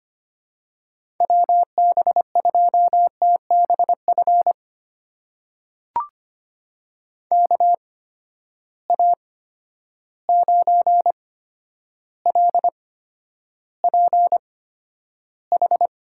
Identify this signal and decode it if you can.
Morse code, heard as a keyed tone, about 25 wpm, spaced as for 11 wpm, 700 Hz. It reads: WB2TBF KA9LPH
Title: USA Call Signs - 6 Characters - 8x Character Spacing - 25wpm